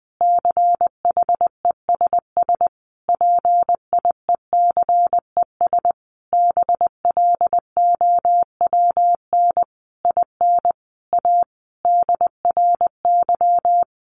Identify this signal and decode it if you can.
Morse code, heard as a keyed tone, about 20 wpm, 700 Hz. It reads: CHESS PIECES BLOWN IN A DRY
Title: A Sound Of Thunder 20wpm